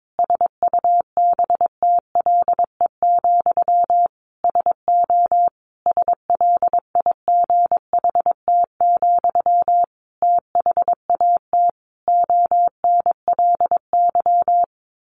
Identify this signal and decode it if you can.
Morse code, heard as a keyed tone, about 22 wpm, 700 Hz. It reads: SUBTLE, SO SLIGHT, THAT ONLY
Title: A Sound Of Thunder 22wpm